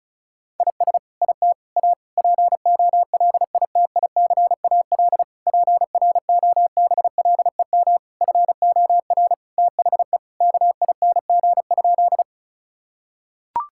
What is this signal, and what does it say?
Morse code, heard as a keyed tone, about 34 wpm, 700 Hz. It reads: IS IT A POLITICAL PROBLEM FOR THE KING?
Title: Sentences from Top 1000 Words - Review Part 5 - 35wpm